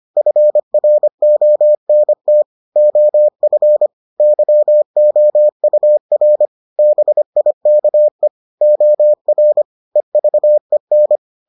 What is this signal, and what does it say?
Morse code, heard as a keyed tone, about 25 wpm, 600 Hz. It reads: FRONT OF YOUR BIKE OR EVEN